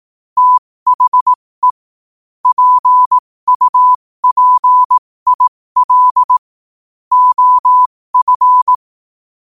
Morse code, heard as a keyed tone, about 18 wpm, spaced as for 15 wpm, 1000 Hz. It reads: THE PUPIL OF
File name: qotdfiles / 15wpm